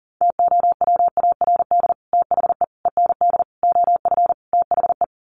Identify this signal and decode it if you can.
Morse code, heard as a keyed tone, about 40 wpm, 700 Hz. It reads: TOWARD THE END OF THE
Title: Sets of 5 Words 40wpm